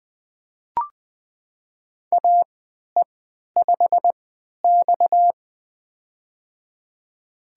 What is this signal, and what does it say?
Morse code, heard as a keyed tone, about 20 wpm, spaced as for 11 wpm, 700 Hz. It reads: AE5X
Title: USA Call Signs - 4 Characters - 3x Character Spacing - 20wpm